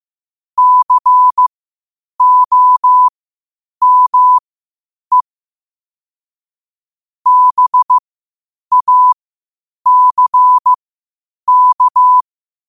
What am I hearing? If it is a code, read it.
Morse code, heard as a keyed tone, about 15 wpm, spaced as for 8 wpm, 1000 Hz. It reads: COME BACK
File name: qotdfiles / 8wpm